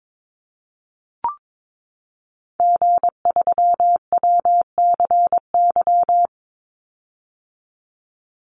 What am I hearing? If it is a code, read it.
Morse code, heard as a keyed tone, about 22 wpm, 700 Hz. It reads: G3WCY
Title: Call Signs - 22wpm